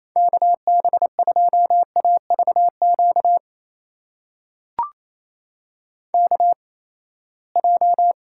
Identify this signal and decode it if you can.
Morse code, heard as a keyed tone, about 28 wpm, spaced as for 19 wpm, 700 Hz. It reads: KB2AVQ KJ
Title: USA Call Signs - 6 Characters - 8x Character Spacing - 28wpm